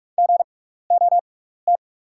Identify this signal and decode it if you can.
Morse code, heard as a keyed tone, about 44 wpm, spaced as for 15 wpm, 690 Hz. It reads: GOT
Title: 6x Character Spacing - Top 300 Words - 45wpm